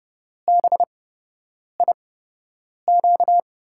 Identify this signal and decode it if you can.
Morse code, heard as a keyed tone, about 30 wpm, spaced as for 8 wpm, 700 Hz. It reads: BIQ
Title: USA Call Signs - 8x Character Spacing - 30wpm